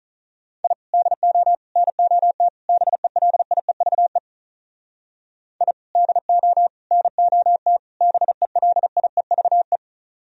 Morse code, heard as a keyed tone, about 45 wpm, 700 Hz. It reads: I DO NOT BELIEVE I DO NOT BELIEVE